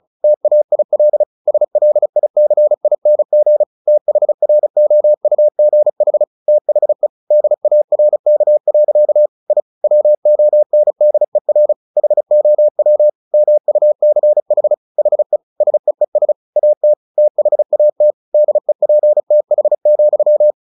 Morse code, heard as a keyed tone, about 36 wpm, 600 Hz. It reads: TAIL SLICING THROUGH THE DARK. I WONDER HOW MUCH HE SEES AT THAT DEPTH,
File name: the old man and the sea / the old man and the sea 1 35wpm